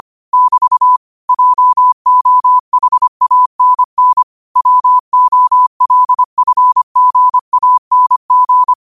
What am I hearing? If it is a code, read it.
Morse code, heard as a keyed tone, about 25 wpm, 1000 Hz. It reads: <BT> JOHANN WOLFGANG
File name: qotdfiles / 25wpm